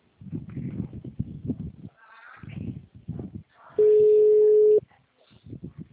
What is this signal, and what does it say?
Ringback tone (Europe, 425 Hz).
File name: out- -2010- - - .81751